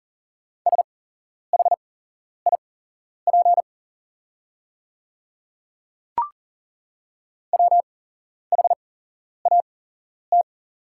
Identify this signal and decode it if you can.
Morse code, heard as a keyed tone, about 38 wpm, spaced as for 11 wpm, 700 Hz. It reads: SHIP WHAT
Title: 8x Character Spacing - Top 500 Words - 40wpm